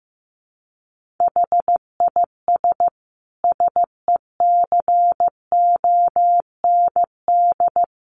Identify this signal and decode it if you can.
Morse code, heard as a keyed tone, about 15 wpm, 700 Hz. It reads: HIS SECOND
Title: Sentences from Top 1400 Words 15wpm